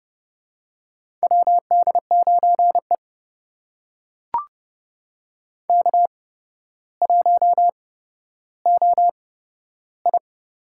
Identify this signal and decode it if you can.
Morse code, heard as a keyed tone, about 30 wpm, spaced as for 13 wpm, 700 Hz. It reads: WD9E K1OI
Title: USA Call Signs - 8x Character Spacing - 30wpm